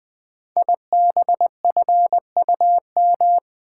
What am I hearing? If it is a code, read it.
Morse code, heard as a keyed tone, about 20 wpm, 700 Hz. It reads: IBFUM
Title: Code Groups - 5 Letter:Numbers - 20wpm